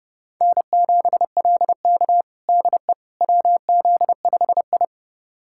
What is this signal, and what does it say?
Morse code, heard as a keyed tone, about 32 wpm, 700 Hz. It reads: N7LK DE WZ5I
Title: USA Callsign DE Callsign - 4 Characters - 30wpm